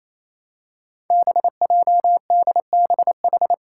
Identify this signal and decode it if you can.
Morse code, heard as a keyed tone, about 28 wpm, 700 Hz. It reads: BJDBH
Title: Five Challenging Letter-Number Combinations 28wpm